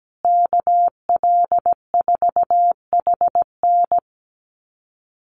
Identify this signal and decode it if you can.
Morse code, heard as a keyed tone, about 17 wpm, 700 Hz. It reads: KL4HN
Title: USA Call Signs Repeated 3x 17wpm